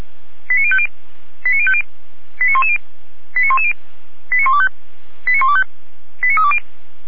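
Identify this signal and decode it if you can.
Five-tone selective call (ZVEI), heard as quick runs of five tones, tones of about 70 ms. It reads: 89050 89050 89110 89110 89125 89125 89320
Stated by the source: Five-tone ZVEI selective call 89187, received by SDR.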